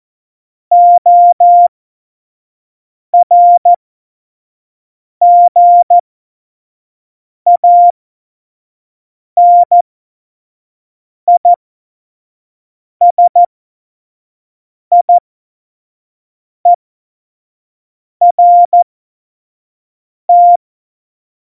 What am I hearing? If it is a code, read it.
Morse code, heard as a keyed tone, about 14 wpm, spaced as for 5 wpm, 700 Hz. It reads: ORGANISIERT
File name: Morsekurs / Witze / Witz 03. November 2020, 25 F70